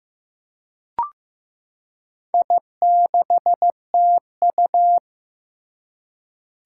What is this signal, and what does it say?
Morse code, heard as a keyed tone, about 15 wpm, 700 Hz. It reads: I6TU
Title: Four Challenging Letter-Number Combinations 15wpm